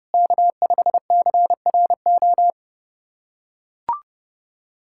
Morse code, heard as a keyed tone, about 30 wpm, 700 Hz. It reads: K5CRO
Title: USA Callsign DE Callsign 30wpm